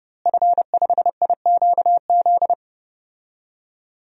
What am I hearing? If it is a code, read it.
Morse code, heard as a keyed tone, about 30 wpm, 700 Hz. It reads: F5IQZ
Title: DXCC-100 Call Signs - 5 Characters - Repeated 2x - 30wpm